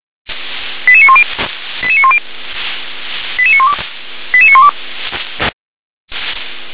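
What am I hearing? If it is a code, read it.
Five-tone selective call (ZVEI), heard as quick runs of five tones, tones of about 70 ms. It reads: 88910 88910 88912 88912
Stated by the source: ZVEI five-tone selective call 88911 received on SDR.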